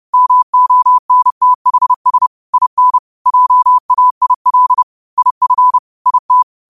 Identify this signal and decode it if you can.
Morse code, heard as a keyed tone, about 30 wpm, 1000 Hz. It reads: MONTHS IN JAIL IF IT
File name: qotdfiles / 30wpm